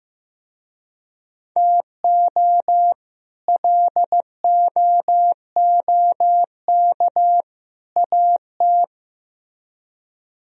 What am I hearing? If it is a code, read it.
Morse code, heard as a keyed tone, about 17 wpm, 700 Hz. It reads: TO LOOK AT
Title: Speed-Racing - Sets of 3 Words 15wpm